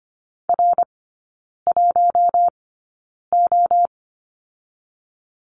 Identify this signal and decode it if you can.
Morse code, heard as a keyed tone, about 25 wpm, spaced as for 9 wpm, 690 Hz. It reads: R1O
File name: Lesson 14e - Number 2 - Three Characters 25wpm